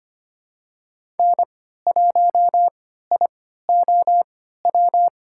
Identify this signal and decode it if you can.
Morse code, heard as a keyed tone, about 25 wpm, spaced as for 14 wpm, 700 Hz. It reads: N1IOW
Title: USA Call Signs - 5 Characters - 3x Character Spacing - 25wpm